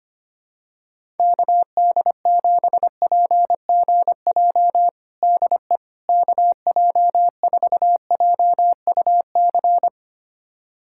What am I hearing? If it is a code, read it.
Morse code, heard as a keyed tone, about 25 wpm, 700 Hz. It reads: KD7PGJ DE KJ4JUC
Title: USA Callsign DE Callsign 25wpm